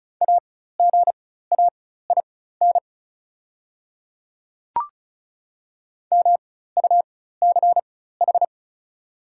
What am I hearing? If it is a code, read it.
Morse code, heard as a keyed tone, about 34 wpm, spaced as for 16 wpm, 690 Hz. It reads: AGAIN MUCH
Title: 4x Character Spacing - Top 200 Words - 35wpm